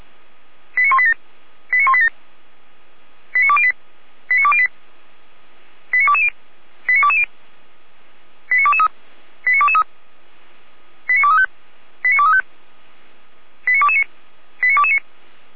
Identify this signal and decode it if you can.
Five-tone selective call (ZVEI), heard as quick runs of five tones, tones of about 70 ms. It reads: 89187 89187 89208 89208 89220 89220 89223 89223 89235 89235 89229 89229